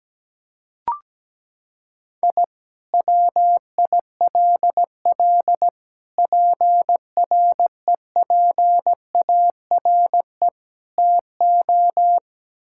Morse code, heard as a keyed tone, about 17 wpm, 700 Hz. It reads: I WILL PREPARE TO
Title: Sentences from Top 1000 Words - Review Part 6 - 17wpm